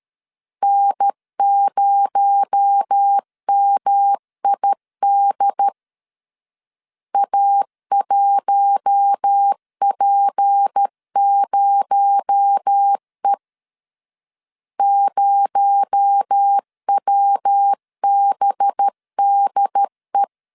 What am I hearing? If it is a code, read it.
Morse code, heard as a keyed tone, about 13 wpm, 800 Hz. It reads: N0MID A1P0E 0WBDE
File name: LEZIONI / 6 PKX0 / 6 PKX0 12wpm 2231